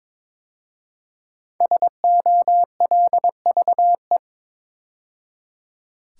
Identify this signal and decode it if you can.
Morse code, heard as a keyed tone, about 23 wpm, 700 Hz. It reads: SOLVE
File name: 5 Letter Words - First 3 Letters Given - 22wpm